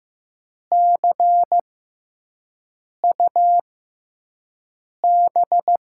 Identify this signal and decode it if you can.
Morse code, heard as a keyed tone, about 15 wpm, spaced as for 5 wpm, 700 Hz. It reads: CUB
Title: USA Call Signs - 5 Characters - 6x Character Spacing - 15wpm